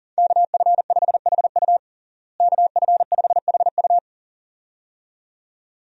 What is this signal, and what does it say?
Morse code, heard as a keyed tone, about 37 wpm, 700 Hz. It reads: KF5HU KF5HU
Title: USA Call Signs - 5 Characters - Repeated 2x - 40wpm